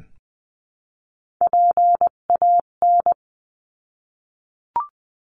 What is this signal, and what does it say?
Morse code, heard as a keyed tone, about 20 wpm, 700 Hz. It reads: PAN
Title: 258 Lesson 42c - Words 16-20wpm